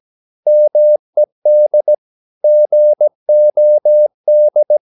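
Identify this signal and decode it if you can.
Morse code, heard as a keyed tone, about 17 wpm, 600 Hz. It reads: MED GOD
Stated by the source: Slow (QRS) Morse practice at 17 wpm, from CW med Gnister.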